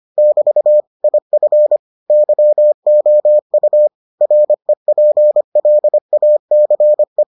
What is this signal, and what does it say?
Morse code, heard as a keyed tone, about 25 wpm, 600 Hz. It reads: <BT> IF YOU REPLACE